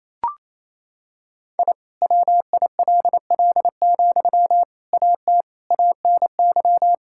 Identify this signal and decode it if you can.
Morse code, heard as a keyed tone, about 27 wpm, 700 Hz. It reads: I WILL, AT ANY
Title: Sentences from Top 1000 Words - Review Part 5 - 28wpm